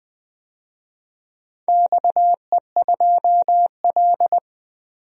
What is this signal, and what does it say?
Morse code, heard as a keyed tone, about 20 wpm, 700 Hz. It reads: XE2L
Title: DXCC-100 Call Signs - 4 Characters - 20wpm